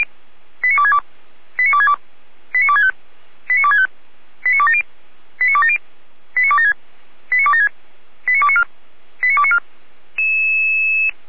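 Five-tone selective call (ZVEI), heard as quick runs of five tones, tones of about 70 ms. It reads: 89262 89262 89265 89265 89270 89270 89276 89276 89294 89294
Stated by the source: Received by SDR, ZVEI five-tone call 89280.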